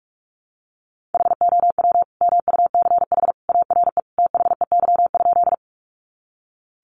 Morse code, heard as a keyed tone, about 44 wpm, 700 Hz. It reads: HOW MUCH ARE THEY?
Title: Sentences from Top 200 Words 45wpm